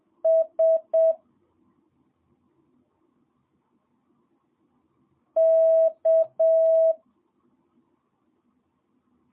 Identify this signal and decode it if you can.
Morse code, heard as a keyed tone, about 7 wpm, 650 Hz. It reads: S K
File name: jh8cbh / 2022cw